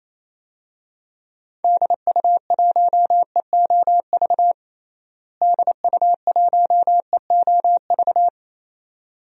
Morse code, heard as a keyed tone, about 28 wpm, 700 Hz. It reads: DU1EOV DU1EOV
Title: DXCC-100 Call Signs - 6 Characters - Repeated 2x - 28wpm